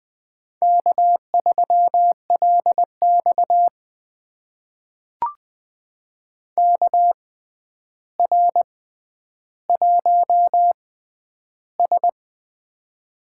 Morse code, heard as a keyed tone, about 20 wpm, spaced as for 10 wpm, 700 Hz. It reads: K3LX KR1S
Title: USA Call Signs - 4 Characters - 6x Character Spacing - 20wpm